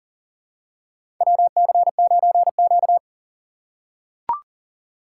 Morse code, heard as a keyed tone, about 40 wpm, 700 Hz. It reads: WC9Q
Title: USA Call Signs - 4 Characters - 3x Character Spacing - 40wpm